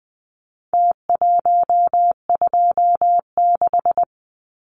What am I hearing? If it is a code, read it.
Morse code, heard as a keyed tone, about 20 wpm, 700 Hz. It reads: T126
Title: Numbers - Contest Style - 4-Digit 20wpm